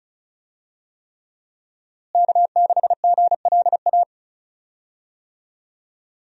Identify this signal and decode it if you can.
Morse code, heard as a keyed tone, about 36 wpm, 700 Hz. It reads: K6GLA
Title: USA Call Signs - 5 Characters - Repeated 3x - 35wpm